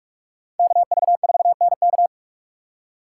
Morse code, heard as a keyed tone, about 43 wpm, 700 Hz. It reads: KV4NK
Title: USA Call Signs - 5 Characters - 2x Character Spacing - 45wpm